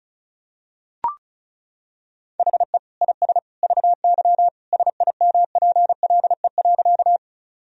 Morse code, heard as a keyed tone, about 34 wpm, 700 Hz. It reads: HE IS VY SIMPLE.
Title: Sentences from Top 400 Words and Abbreviations 35wpm